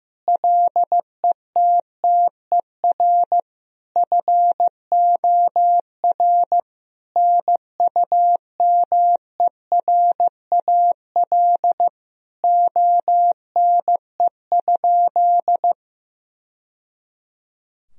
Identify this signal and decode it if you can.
Morse code, heard as a keyed tone, about 15 wpm, 700 Hz. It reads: LETTER FOR NUMERAL ONE?
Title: Sentences from Top 1000 Words - Review Part 2 - 15wpm